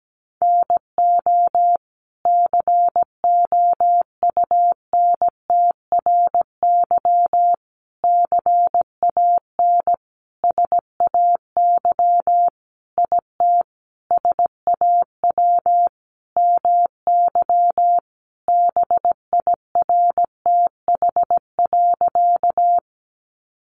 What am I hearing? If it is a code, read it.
Morse code, heard as a keyed tone, about 17 wpm, 700 Hz. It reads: NO COUNTRY CAN SAY IT SAW MY BIRTH.
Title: Sentences from Top 800 Words 17wpm